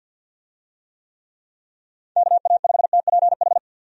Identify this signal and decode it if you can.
Morse code, heard as a keyed tone, about 50 wpm, 700 Hz. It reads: KA5TPH
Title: USA Call Signs - 6 Characters - 50wpm